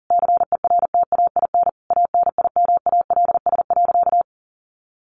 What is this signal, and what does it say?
Morse code, heard as a keyed tone, about 40 wpm, 700 Hz. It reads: CERTAIN ANIMALS.
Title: A Sound Of Thunder 40wpm